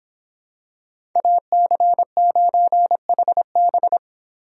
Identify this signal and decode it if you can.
Morse code, heard as a keyed tone, about 26 wpm, 700 Hz. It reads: AC9HB